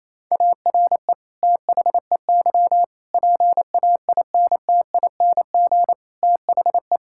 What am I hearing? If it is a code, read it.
Morse code, heard as a keyed tone, about 28 wpm, 700 Hz. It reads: ARE THEY PAINTING THE
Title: Sentences from Top 1000 Words - Review Part 6 - 28wpm